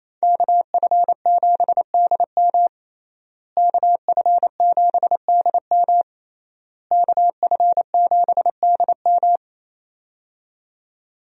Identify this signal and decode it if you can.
Morse code, heard as a keyed tone, about 28 wpm, 700 Hz. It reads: KF7DM KF7DM KF7DM
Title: USA Call Signs - 5 Characters - Repeated 3x - 28wpm